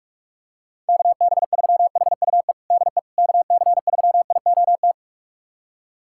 Morse code, heard as a keyed tone, about 46 wpm, 700 Hz. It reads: KB3HUE DE KC3IOT
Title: USA Callsign DE Callsign - 6 Characters - 45wpm